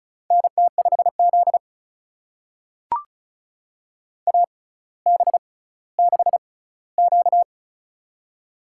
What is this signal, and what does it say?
Morse code, heard as a keyed tone, about 36 wpm, spaced as for 18 wpm, 700 Hz. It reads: NT5Z AB6Q
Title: USA Call Signs - 4 Characters - 6x Character Spacing - 35wpm